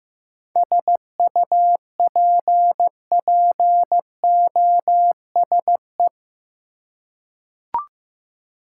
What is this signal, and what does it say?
Morse code, heard as a keyed tone, about 15 wpm, 700 Hz. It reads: SUPPOSE
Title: Seven Letter Words - 15wpm